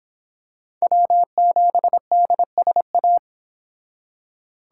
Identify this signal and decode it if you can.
Morse code, heard as a keyed tone, about 26 wpm, 700 Hz. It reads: W7DSA